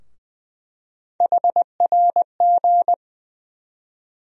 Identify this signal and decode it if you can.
Morse code, heard as a keyed tone, about 20 wpm, 700 Hz. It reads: HRG
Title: USA Call Sign Postfixes - 3 Letters 20wpm